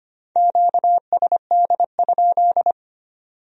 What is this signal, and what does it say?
Morse code, heard as a keyed tone, about 32 wpm, 700 Hz. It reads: QSD?